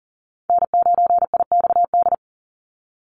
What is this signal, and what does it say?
Morse code, heard as a keyed tone, about 40 wpm, 700 Hz. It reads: N9IXD
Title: Call Signs Repeated 2x - 40wpm